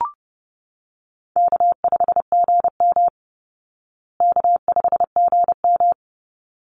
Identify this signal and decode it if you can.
Morse code, heard as a keyed tone, about 28 wpm, 700 Hz. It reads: K5GM K5GM
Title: Easy Call Signs Repeated 2x - 30wpm